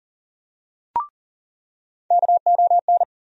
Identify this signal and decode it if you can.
Morse code, heard as a keyed tone, about 40 wpm, 700 Hz. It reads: KON
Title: USA Call Sign Postfixes - 3 Letters 40wpm